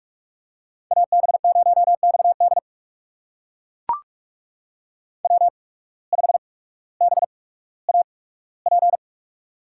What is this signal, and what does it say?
Morse code, heard as a keyed tone, about 44 wpm, spaced as for 19 wpm, 700 Hz. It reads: AB0XD W5BAP
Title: USA Call Signs - 5 Characters - 8x Character Spacing - 45wpm